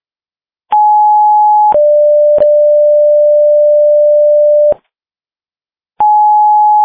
Two-tone page, heard as a one-second tone, then a three-second tone, about 840 then 590 Hz. It is heard from 0.7 to 4.7 s.